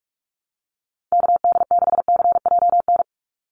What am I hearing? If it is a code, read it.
Morse code, heard as a keyed tone, about 44 wpm, 700 Hz. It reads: KD6CJN
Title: USA Call Signs 45wpm